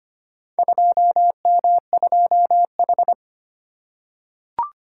Morse code, heard as a keyed tone, about 25 wpm, 700 Hz. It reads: 2M2H
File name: practice / 25wpm / Four Letter-Number Combinations - 25wpm